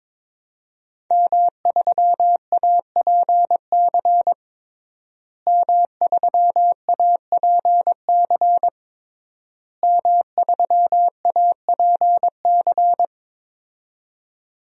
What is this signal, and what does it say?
Morse code, heard as a keyed tone, about 22 wpm, 700 Hz. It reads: M3APC M3APC M3APC
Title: DXCC-100 Call Signs - 5 Characters - Repeated 3x - 22wpm